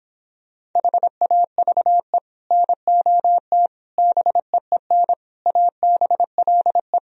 Morse code, heard as a keyed tone, about 26 wpm, 700 Hz. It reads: HAVE NOT BEEN ABLE